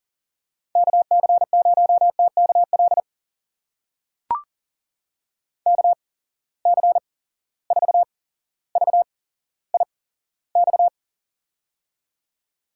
Morse code, heard as a keyed tone, about 40 wpm, spaced as for 17 wpm, 700 Hz. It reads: KC0TKL KC4VIX
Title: USA Call Signs - 6 Characters - 8x Character Spacing - 40wpm